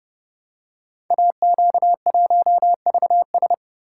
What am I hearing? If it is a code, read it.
Morse code, heard as a keyed tone, about 30 wpm, 700 Hz. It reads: AQ1VS
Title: Code Groups - 5 Letter:Numbers - 30wpm